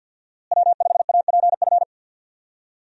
Morse code, heard as a keyed tone, about 49 wpm, 700 Hz. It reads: W5APF
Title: USA Call Signs - 5 Characters - Repeated 3x - 50wpm